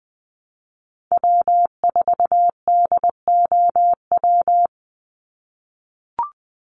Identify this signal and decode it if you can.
Morse code, heard as a keyed tone, about 20 wpm, 700 Hz. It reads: W4DOW
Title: NA SOTA Chaser Callsigns 20wpm